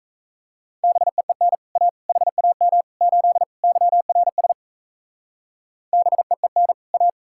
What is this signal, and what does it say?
Morse code, heard as a keyed tone, about 34 wpm, 700 Hz. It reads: BEEN A HAM 8 YRS BEEN A
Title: Speed-Racing Common QSO Phrases - 28wpm